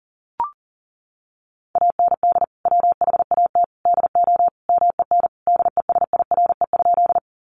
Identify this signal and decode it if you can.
Morse code, heard as a keyed tone, about 40 wpm, 700 Hz. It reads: AND WHAT DO MEN DESIRE?
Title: Sentences from Top 1100 Words 40wpm